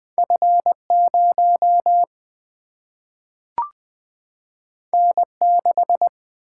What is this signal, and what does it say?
Morse code, heard as a keyed tone, about 20 wpm, 700 Hz. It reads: F0 N6
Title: Single-Letter Single-Number 20wpm